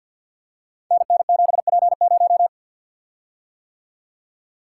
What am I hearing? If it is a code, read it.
Morse code, heard as a keyed tone, about 53 wpm, 700 Hz. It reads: NN7P0
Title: Code Groups - 5 Letter:Numbers - 50wpm